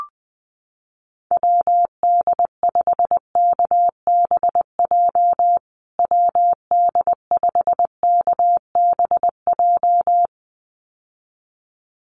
Morse code, heard as a keyed tone, about 20 wpm, 700 Hz. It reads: WD5KBJ WD5KBJ
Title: USA Call Signs Repeated 2x 20wpm